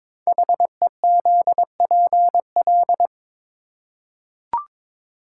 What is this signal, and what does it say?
Morse code, heard as a keyed tone, about 22 wpm, 700 Hz. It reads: HEZPL
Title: Code Groups - 5 Letters - 22wpm